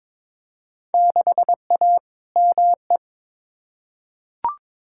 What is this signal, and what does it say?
Morse code, heard as a keyed tone, about 21 wpm, 700 Hz. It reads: 6A ME
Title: Field Day Exchanges - 22wpm